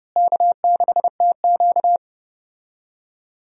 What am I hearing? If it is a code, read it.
Morse code, heard as a keyed tone, about 32 wpm, 700 Hz. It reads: K6TQ
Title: USA Call Signs - 4 Characters - Repeated 3x - 30wpm